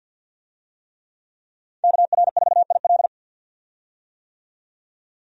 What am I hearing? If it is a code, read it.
Morse code, heard as a keyed tone, about 53 wpm, 700 Hz. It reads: KR4IL
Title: USA Call Signs - 5 Characters - 50wpm